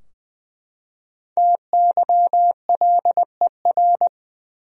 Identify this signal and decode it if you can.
Morse code, heard as a keyed tone, about 20 wpm, 700 Hz. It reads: TYLER
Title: USA Common Names - Max 5 Letters - 20wpm